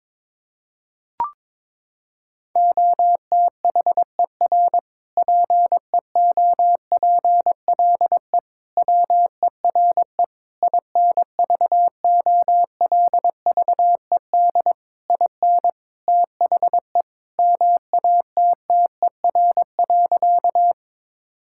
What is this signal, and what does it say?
Morse code, heard as a keyed tone, about 22 wpm, 700 Hz. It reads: OTHER PEOPLE WERE INVOLVED IN THE MATTER.
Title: Sentences from Top 1000 Words - Review Part 5 - 22wpm